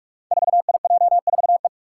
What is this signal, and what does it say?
Morse code, heard as a keyed tone, about 43 wpm, 700 Hz. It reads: 4IJ4E
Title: Five Challenging Letter-Number Combinations 45wpm